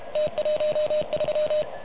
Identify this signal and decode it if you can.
Morse code, heard as a keyed tone, about 30 wpm, 590 Hz. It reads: T13
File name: truSDX / MWC29.8